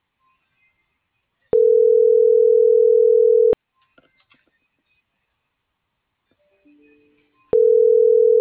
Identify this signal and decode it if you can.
Ringback tone (North America), 440 + 480 Hz.